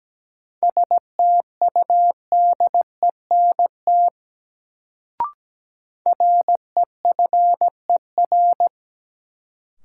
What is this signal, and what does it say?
Morse code, heard as a keyed tone, about 17 wpm, 700 Hz. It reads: STUDENT REFER
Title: Sentences from Top 1000 Words - New Word Drill - 17wpm